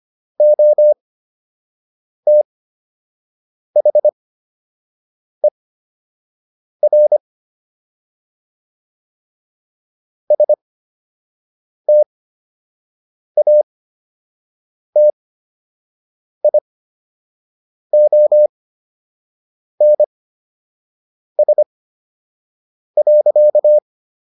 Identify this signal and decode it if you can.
Morse code, heard as a keyed tone, about 25 wpm, spaced as for 6 wpm, 600 Hz. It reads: OTHER STATIONS.